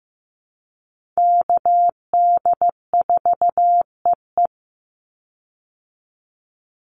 Morse code, heard as a keyed tone, about 15 wpm, 700 Hz. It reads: KD4EE
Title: Call Signs Repeated 3x - 15wpm